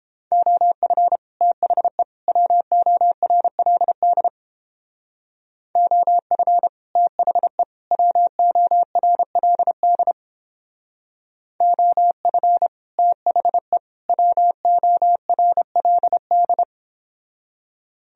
Morse code, heard as a keyed tone, about 29 wpm, 700 Hz. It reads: OF THE WORLD OF THE WORLD OF THE WORLD